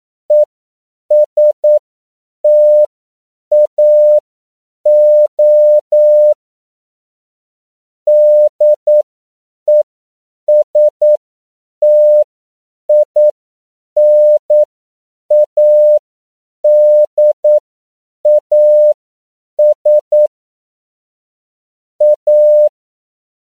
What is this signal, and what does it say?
Morse code, heard as a keyed tone, about 9 wpm, spaced as for 7 wpm, 600 Hz. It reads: ESTAO DESTINADAS A
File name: CW / 10PPM